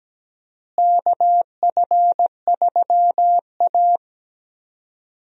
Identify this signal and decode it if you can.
Morse code, heard as a keyed tone, about 17 wpm, 700 Hz. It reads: KF3A
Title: USA Call Signs - 4 Characters - Repeated 2x - 17wpm